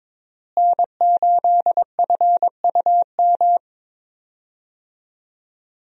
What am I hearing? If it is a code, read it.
Morse code, heard as a keyed tone, about 22 wpm, 700 Hz. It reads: N8FUM
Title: USA Call Signs - 5 Characters - Repeated 2x - 22wpm